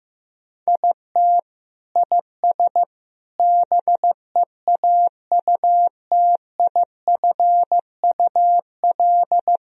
Morse code, heard as a keyed tone, about 15 wpm, 700 Hz. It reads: IT IS BEAUTIFUL
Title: Sentences from Top 1000 Words - Review Part 1 - 15wpm